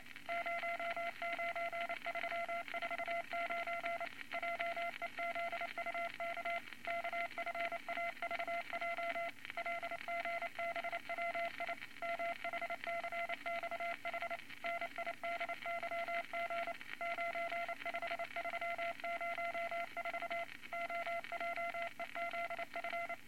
Morse code, heard as a keyed tone, about 28 wpm, 680 Hz, with noise. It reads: 09349 JEZUK KFAVJ LGBWI MHCXH NIDYG 95304 OJEZF
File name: Morse Code Records / 22 A LCM 172 Twenty Words Per Minute